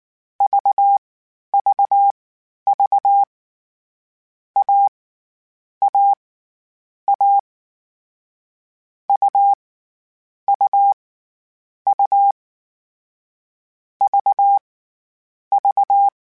Morse code, heard as a keyed tone, about 19 wpm, spaced as for 11 wpm, 800 Hz. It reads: VVV A A A U U U V V